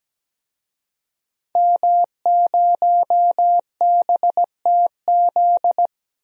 Morse code, heard as a keyed tone, about 17 wpm, 700 Hz. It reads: M0BTZ
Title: DXCC-100 Call Signs - 5 Characters - Repeated 3x - 17wpm